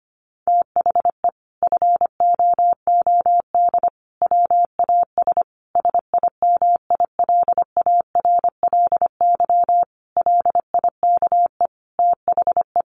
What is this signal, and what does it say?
Morse code, heard as a keyed tone, about 25 wpm, 700 Hz. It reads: THE FOOD WAS SIMILARLY LIKE THE
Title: Sentences from Top 1400 Words 25wpm